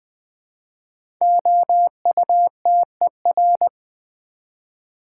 Morse code, heard as a keyed tone, about 20 wpm, 700 Hz. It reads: OUTER
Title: 5 Letter Words - First 3 Letters Given - 20wpm